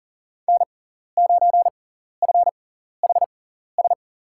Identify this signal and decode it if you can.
Morse code, heard as a keyed tone, about 37 wpm, spaced as for 14 wpm, 700 Hz. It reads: N9FHS
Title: USA Call Signs - 5 Characters - 6x Character Spacing - 40wpm